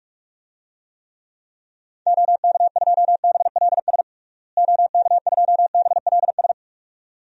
Morse code, heard as a keyed tone, about 47 wpm, 700 Hz. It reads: OK2BLS OK2BLS
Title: DXCC-100 Call Signs - 6 Characters - Repeated 2x - 45wpm